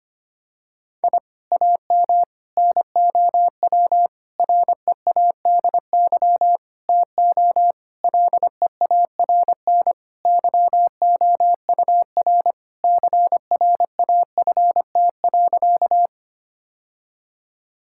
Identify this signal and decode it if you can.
Morse code, heard as a keyed tone, about 25 wpm, 700 Hz. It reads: I AM NOW READY TO LEARN YOUR CRAFT.
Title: Sentences from Top 1000 Words - Review Part 6 - 25wpm